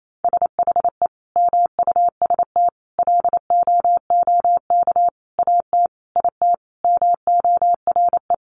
Morse code, heard as a keyed tone, about 28 wpm, 700 Hz. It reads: SHE MUST LOOK AT IT MORE
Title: Sentences from Top 1400 Words 28wpm